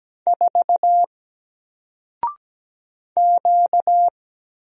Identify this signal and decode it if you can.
Morse code, heard as a keyed tone, about 17 wpm, 700 Hz. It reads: 4 Q
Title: Single Character (Letter:Number:Punctuation:Prosign) 17wpm